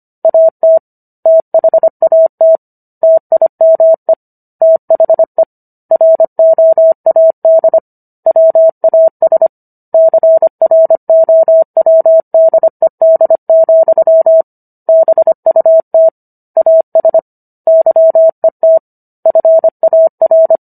Morse code, heard as a keyed tone, about 25 wpm, 650 Hz. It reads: AT THAT TIME THE ROAD WAS CROWDED, BUT AS YET FAR